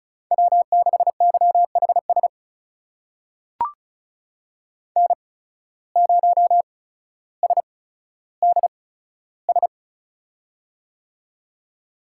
Morse code, heard as a keyed tone, about 36 wpm, spaced as for 15 wpm, 700 Hz. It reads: W6YHS N0SDS
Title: USA Call Signs - 5 Characters - 8x Character Spacing - 35wpm